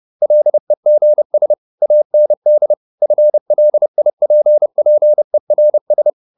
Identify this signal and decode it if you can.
Morse code, heard as a keyed tone, about 30 wpm, 600 Hz. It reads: LEGS AND FLIPPERS